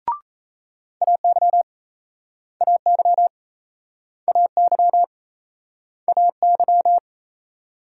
Morse code, heard as a keyed tone, about 39 wpm, 700 Hz. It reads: AY AY AY AY